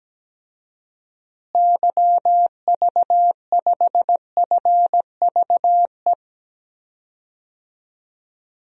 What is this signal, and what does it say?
Morse code, heard as a keyed tone, about 17 wpm, 700 Hz. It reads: YV5FVE
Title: DXCC-100 Call Signs - 6 Characters - Repeated 3x - 17wpm